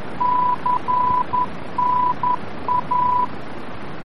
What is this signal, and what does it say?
Morse code, heard as a keyed tone, about 11 wpm, 990 Hz, with noise. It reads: CNA
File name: france / mx / CNA-VOR-F